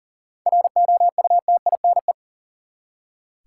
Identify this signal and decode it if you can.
Morse code, heard as a keyed tone, about 40 wpm, 700 Hz. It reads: ROUTINE
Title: Seven Letter Words - 40wpm